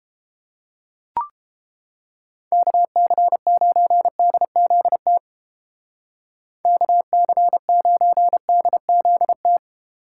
Morse code, heard as a keyed tone, about 29 wpm, 700 Hz. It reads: KC9DZT KC9DZT